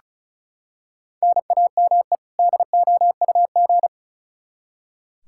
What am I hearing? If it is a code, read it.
Morse code, heard as a keyed tone, about 36 wpm, 700 Hz. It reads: NAME DOUG
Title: Practice QSO - Exchange 1 - Beginning - 35wpm